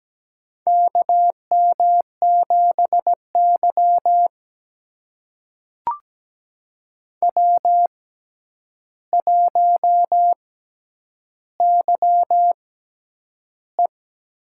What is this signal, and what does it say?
Morse code, heard as a keyed tone, about 17 wpm, spaced as for 9 wpm, 700 Hz. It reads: KM7Y W1YE
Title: USA Call Signs - 4 Characters - 6x Character Spacing - 17wpm